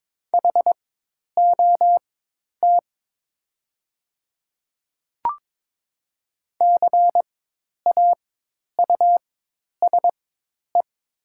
Morse code, heard as a keyed tone, about 22 wpm, spaced as for 10 wpm, 700 Hz. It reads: HOT CAUSE
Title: 4x Character Spacing - Top 200 Words - 22wpm